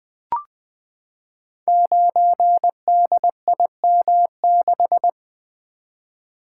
Morse code, heard as a keyed tone, about 20 wpm, 700 Hz. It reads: 9DIM6
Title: Code Groups - 5 Letter:Numbers - 20wpm